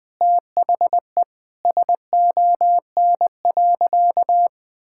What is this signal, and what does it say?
Morse code, heard as a keyed tone, about 21 wpm, 700 Hz. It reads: THE SON.
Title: Sentences with 9-Word Phrases - 20wpm